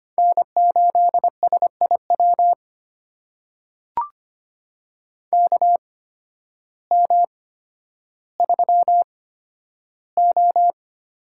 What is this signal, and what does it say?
Morse code, heard as a keyed tone, about 25 wpm, spaced as for 12 wpm, 700 Hz. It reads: N8SIW KM3O
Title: USA Call Signs - 8x Character Spacing - 25wpm